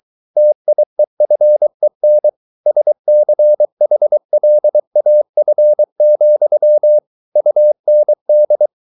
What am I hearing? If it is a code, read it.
Morse code, heard as a keyed tone, about 23 wpm, 600 Hz. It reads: TIEFEN SCHLAF, UND